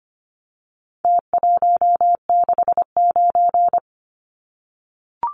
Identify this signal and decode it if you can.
Morse code, heard as a keyed tone, about 25 wpm, 700 Hz. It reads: T169
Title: Numbers - Contest Style - 4-Digit 25wpm